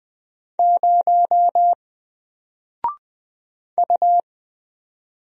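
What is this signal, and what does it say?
Morse code, heard as a keyed tone, about 20 wpm, 700 Hz. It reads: 0 U
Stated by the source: Single letter and number Morse drill at 20 wpm, from N4LCD.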